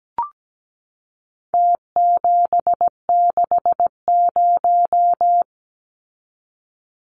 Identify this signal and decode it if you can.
Morse code, heard as a keyed tone, about 17 wpm, 700 Hz. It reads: T760
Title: Numbers - Contest Style - 4-Digit 17wpm